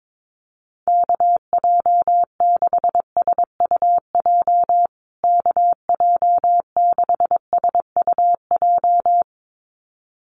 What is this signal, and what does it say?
Morse code, heard as a keyed tone, about 22 wpm, 700 Hz. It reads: KJ6SUJ KJ6SUJ
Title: USA Call Signs Repeated 2x 22wpm